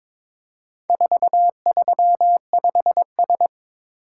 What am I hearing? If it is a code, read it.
Morse code, heard as a keyed tone, about 22 wpm, 700 Hz. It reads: 435S
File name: practice / 22wpm / Four Letter-Number Combinations - 22wpm